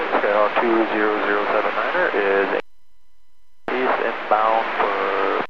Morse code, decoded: none

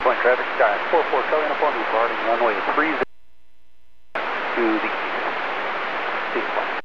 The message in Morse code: none

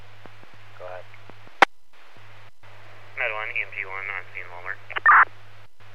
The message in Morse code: none

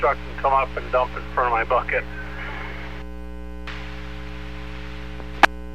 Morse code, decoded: none

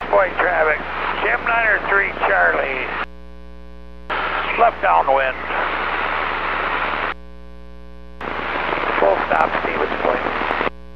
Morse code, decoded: none

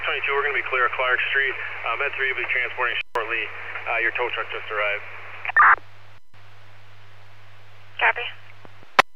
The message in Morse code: none

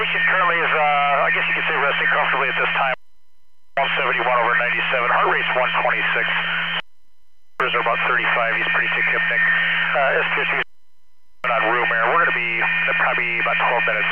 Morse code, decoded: none

